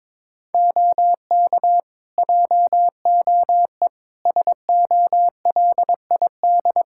Morse code, heard as a keyed tone, about 22 wpm, 700 Hz. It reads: OK JOE SOLID
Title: Practice QSO - Exchange 3 - More Info - 22wpm